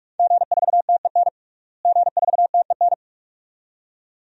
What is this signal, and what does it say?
Morse code, heard as a keyed tone, about 46 wpm, 700 Hz. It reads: G4TEN G4TEN
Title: DXCC-100 Call Signs - 5 Characters - Repeated 2x - 45wpm